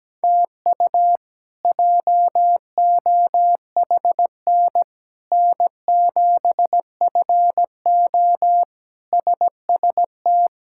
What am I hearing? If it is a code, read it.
Morse code, heard as a keyed tone, about 17 wpm, 700 Hz. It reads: TU JOHN N7FO SST
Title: K1USN SST QSO Practice - 17wpm